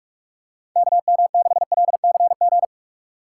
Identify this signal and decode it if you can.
Morse code, heard as a keyed tone, about 43 wpm, 700 Hz. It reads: KM6LCG